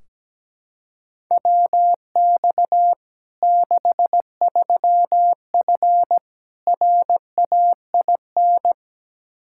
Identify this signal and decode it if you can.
Morse code, heard as a keyed tone, about 17 wpm, 700 Hz. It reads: WX 63F RAIN
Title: Practice QSO - Exchange 2 - Info - 17wpm